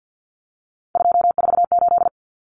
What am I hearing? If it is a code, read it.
Morse code, heard as a keyed tone, about 48 wpm, 700 Hz. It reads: 248
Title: Numbers - 3-Digit 50wpm